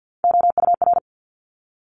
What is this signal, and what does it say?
Morse code, heard as a keyed tone, about 50 wpm, 700 Hz. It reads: OUR
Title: Top Three Letter Combinations in Words 50wpm